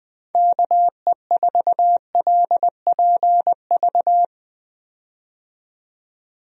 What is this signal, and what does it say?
Morse code, heard as a keyed tone, about 20 wpm, 700 Hz. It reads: KE4LPV
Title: USA Call Signs - 6 Characters - 20wpm